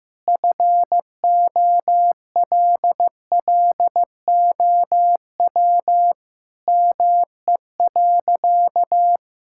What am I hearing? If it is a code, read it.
Morse code, heard as a keyed tone, about 15 wpm, 700 Hz. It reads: FOLLOW ME.
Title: Sentences from Top 1000 Words - Review Part 6 - 15wpm